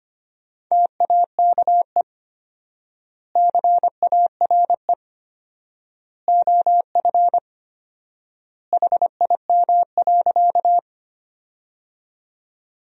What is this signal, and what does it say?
Morse code, heard as a keyed tone, about 25 wpm, 700 Hz. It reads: TAKE CARE OF HIM.
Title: 4x Word-Spacing - Sentences from Top 300 Words - 25wpm